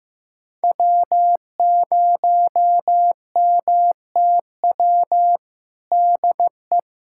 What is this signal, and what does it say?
Morse code, heard as a keyed tone, about 15 wpm, 700 Hz. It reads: W0MTW DE
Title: USA Callsign DE Callsign - 5 Characters - 15wpm